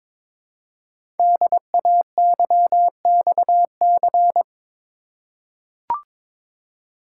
Morse code, heard as a keyed tone, about 22 wpm, 700 Hz. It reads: DAYXC
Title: Code Groups - 5 Letters - 22wpm